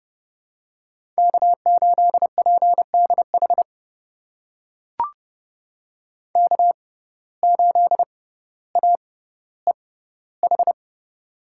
Morse code, heard as a keyed tone, about 30 wpm, spaced as for 15 wpm, 700 Hz. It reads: K8PDH K8AEH
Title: USA Call Signs - 5 Characters - 6x Character Spacing - 30wpm